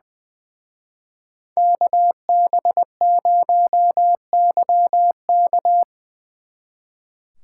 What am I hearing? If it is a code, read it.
Morse code, heard as a keyed tone, about 24 wpm, 700 Hz. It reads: KB0YK
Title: Speed Racing - USA 2x2 Call Signs 20wpm